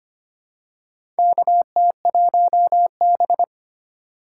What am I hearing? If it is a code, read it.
Morse code, heard as a keyed tone, about 25 wpm, 700 Hz. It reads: KT1B
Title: K1USN SST QSO Practice - 25wpm